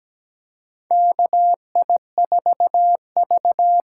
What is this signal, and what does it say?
Morse code, heard as a keyed tone, about 17 wpm, 700 Hz. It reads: KI4V
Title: USA Call Signs - 6 Characters - 8x Character Spacing - 17wpm